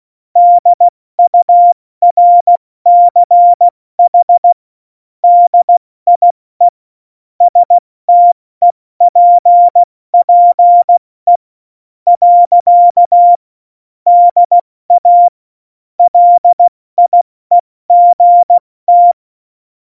Morse code, heard as a keyed tone, about 16 wpm, spaced as for 14 wpm, 700 Hz. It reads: DURCH DIE STEPPE. DA LIEGT
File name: Morsekurs / Witze / Witz 15. Mai 2018, 70 F80